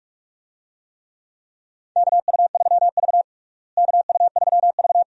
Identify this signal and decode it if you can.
Morse code, heard as a keyed tone, about 47 wpm, 700 Hz. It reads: KU3V KU3V
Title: USA Call Signs - 4 Characters - Repeated 2x - 45wpm